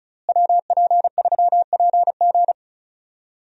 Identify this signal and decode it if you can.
Morse code, heard as a keyed tone, about 34 wpm, 700 Hz. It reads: WP3PG
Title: USA Call Signs - 5 Characters - Repeated 3x - 35wpm